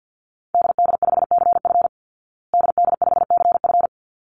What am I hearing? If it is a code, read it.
Morse code, heard as a keyed tone, about 48 wpm, 700 Hz. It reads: DD5CF DD5CF
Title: Call Signs Repeated 3x - 50wpm